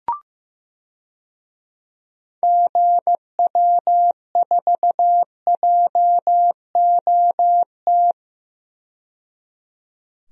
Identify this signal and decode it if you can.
Morse code, heard as a keyed tone, about 16 wpm, 700 Hz. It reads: GW4JOT
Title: DXCC-100 Call Signs - 6 Characters - 15wpm